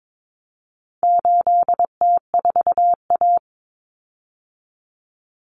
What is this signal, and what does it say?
Morse code, heard as a keyed tone, about 22 wpm, 700 Hz. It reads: 8T4A
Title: Common Cut Numbers - 4-Digit 22wpm